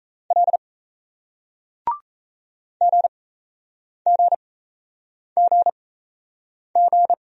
Morse code, heard as a keyed tone, about 38 wpm, spaced as for 8 wpm, 700 Hz. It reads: L GGGG